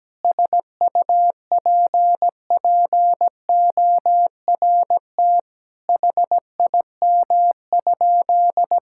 Morse code, heard as a keyed tone, about 17 wpm, 700 Hz. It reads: SUPPORT HIM?
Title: Sentences from Top 1000 Words - Review Part 5 - 17wpm